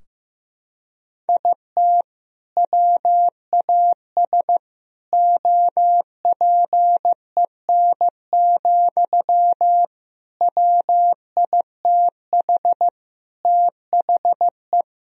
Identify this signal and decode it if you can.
Morse code, heard as a keyed tone, about 16 wpm, 700 Hz. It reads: IT WAS OPEN, WITH THE
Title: Sentences from Top 1000 Words - Review Part 4 - 15wpm